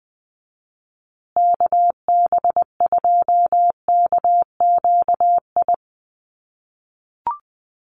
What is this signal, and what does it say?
Morse code, heard as a keyed tone, about 20 wpm, 700 Hz. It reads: KB2KQI
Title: USA Call Signs Repeated 3x 20wpm